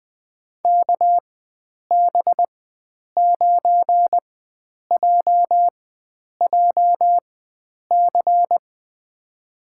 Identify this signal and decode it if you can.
Morse code, heard as a keyed tone, about 20 wpm, spaced as for 9 wpm, 700 Hz. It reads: KB9JJC
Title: USA Call Signs - 6 Characters - 4x Character Spacing - 20wpm